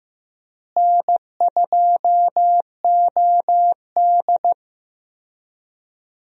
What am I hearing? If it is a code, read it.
Morse code, heard as a keyed tone, about 15 wpm, 700 Hz. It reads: N2OD
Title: USA Call Signs - 4 Characters - Repeated 3x - 15wpm